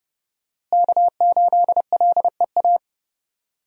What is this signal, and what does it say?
Morse code, heard as a keyed tone, about 30 wpm, 700 Hz. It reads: K8LEA
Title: USA Call Signs - 5 Characters - Repeated 3x - 30wpm